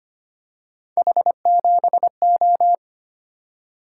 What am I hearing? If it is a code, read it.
Morse code, heard as a keyed tone, about 25 wpm, 700 Hz. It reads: H7O
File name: Three Letter-Number Combinations - 25wpm